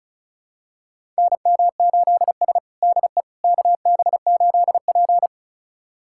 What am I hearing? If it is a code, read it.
Morse code, heard as a keyed tone, about 36 wpm, 700 Hz. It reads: NM8S DE KB8P
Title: USA Callsign DE Callsign - 4 Characters - 35wpm